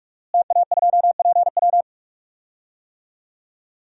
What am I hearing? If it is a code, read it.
Morse code, heard as a keyed tone, about 41 wpm, 700 Hz. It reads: TA2PW
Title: DXCC-100 Call Signs - 5 Characters - Repeated 2x - 45wpm